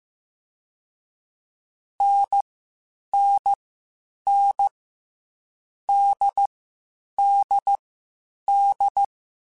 Morse code, heard as a keyed tone, about 15 wpm, 800 Hz. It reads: N N N D D D
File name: Difficult Combos / Difficult Combos-mp3s / NDB NDB6 @13